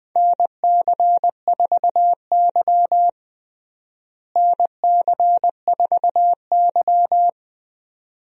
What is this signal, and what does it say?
Morse code, heard as a keyed tone, about 21 wpm, 700 Hz. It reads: NC4Y NC4Y
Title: USA Call Signs - 4 Characters - Repeated 2x - 20wpm